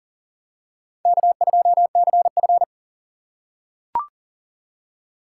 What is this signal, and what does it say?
Morse code, heard as a keyed tone, about 40 wpm, 700 Hz. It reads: K2CF
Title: USA Call Signs - 4 Characters - 3x Character Spacing - 40wpm